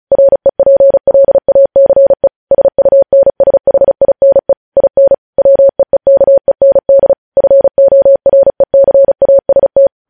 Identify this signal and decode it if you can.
Morse code, heard as a keyed tone, about 36 wpm, 550 Hz. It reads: REPLACE SUNSHINE IN WEEKEND FORECAST
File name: podcast / News headlines-1x-35x35